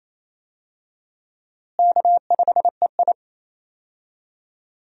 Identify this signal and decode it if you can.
Morse code, heard as a keyed tone, about 27 wpm, 700 Hz. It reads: K5EI
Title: USA Call Signs - 4 Characters - Repeated 3x - 28wpm